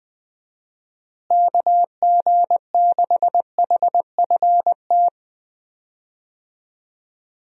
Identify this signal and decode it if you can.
Morse code, heard as a keyed tone, about 20 wpm, 700 Hz. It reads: KG6HFT
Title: USA Call Signs - 6 Characters - Repeated 3x - 20wpm